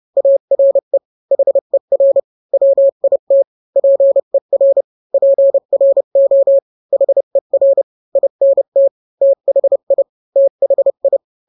Morse code, heard as a keyed tone, about 30 wpm, 550 Hz. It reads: ARE HER WIT PER PRO HER INT THI THI